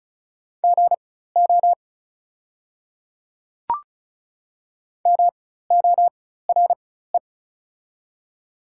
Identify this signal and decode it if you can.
Morse code, heard as a keyed tone, about 36 wpm, spaced as for 16 wpm, 700 Hz. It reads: GO MORE
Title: 4x Character Spacing - Top 100 Words - 35wpm